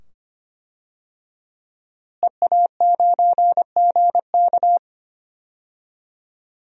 Morse code, heard as a keyed tone, about 25 wpm, 700 Hz. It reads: EA9GK
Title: DXCC-100 Call Signs - 5 Characters - Repeated 2x - 25wpm